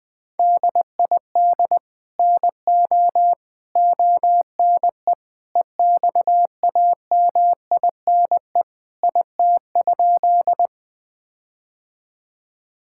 Morse code, heard as a keyed tone, about 20 wpm, 700 Hz. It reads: DID NO ONE EXAMINE IT?
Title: Sentences from Top 1600 Words 20wpm